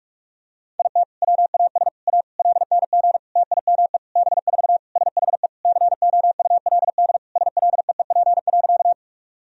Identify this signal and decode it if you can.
Morse code, heard as a keyed tone, about 46 wpm, 700 Hz. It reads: IT WAS A LNG TIME B4 SHE COULD SLEEP.
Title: Sentences from Top 500 Words and Abbreviations 45wpm